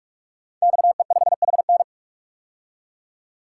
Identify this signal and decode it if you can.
Morse code, heard as a keyed tone, about 44 wpm, 700 Hz. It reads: XE5HN
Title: Code Groups - 5 Letter:Numbers - 45wpm